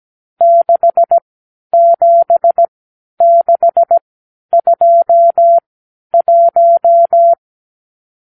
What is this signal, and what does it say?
Morse code, heard as a keyed tone, about 17 wpm, spaced as for 10 wpm, 680 Hz. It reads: 67621
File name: k7qo / Lesson 54 (9)